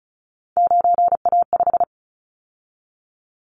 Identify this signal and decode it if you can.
Morse code, heard as a keyed tone, about 34 wpm, 700 Hz. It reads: 9A5
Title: Callsign Prefixes for Top 100 DXCC - 35wpm